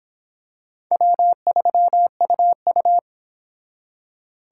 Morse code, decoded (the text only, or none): W3UU